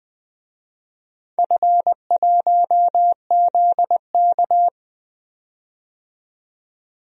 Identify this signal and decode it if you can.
Morse code, heard as a keyed tone, about 21 wpm, 700 Hz. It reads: F1ZK
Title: DXCC-100 Call Signs - 4 Characters - 20wpm